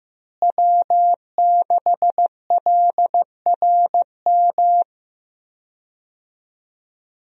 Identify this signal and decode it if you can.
Morse code, heard as a keyed tone, about 15 wpm, 700 Hz. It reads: W6LRM
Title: USA Call Signs - 5 Characters - Repeated 3x - 15wpm